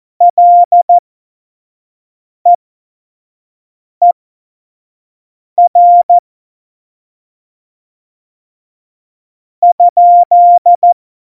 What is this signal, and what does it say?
Morse code, heard as a keyed tone, about 14 wpm, spaced as for 4 wpm, 700 Hz. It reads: LEER?
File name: Morsekurs / Witze / Witz 07. Mai 2019, 25 F70